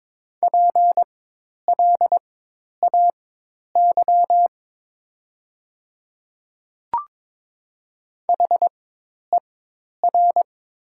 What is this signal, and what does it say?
Morse code, heard as a keyed tone, about 21 wpm, spaced as for 10 wpm, 690 Hz. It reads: PLAY HER
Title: 4x Character Spacing - Top 200 Words - 22wpm